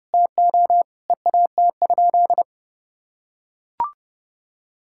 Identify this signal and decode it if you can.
Morse code, heard as a keyed tone, about 30 wpm, 700 Hz. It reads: TO EAT?
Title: Sentences from Top 1600 Words 30wpm